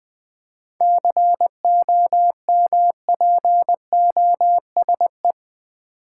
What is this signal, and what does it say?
Morse code, heard as a keyed tone, about 20 wpm, 690 Hz. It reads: COMPOSE
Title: 7 Letter Words - First 3 Letters Given - 20wpm